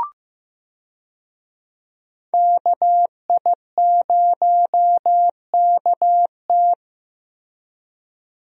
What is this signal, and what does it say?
Morse code, heard as a keyed tone, about 15 wpm, 700 Hz. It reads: KI0KT
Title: USA Call Signs - 5 Characters - Repeated 3x - 15wpm